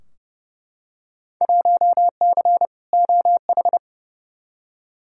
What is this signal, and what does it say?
Morse code, heard as a keyed tone, about 30 wpm, 700 Hz. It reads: 1C OH
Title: Field Day Exchanges - 30wpm